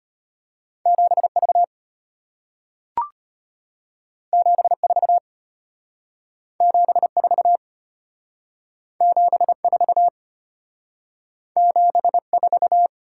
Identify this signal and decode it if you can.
Morse code, heard as a keyed tone, about 34 wpm, 700 Hz. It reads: 7V 74 74 74 74